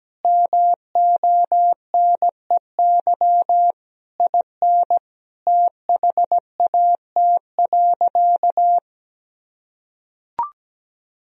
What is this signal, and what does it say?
Morse code, heard as a keyed tone, about 17 wpm, 700 Hz. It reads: MONEY IN THAT.
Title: Sentences from Top 1000 Words - Review Part 4 - 17wpm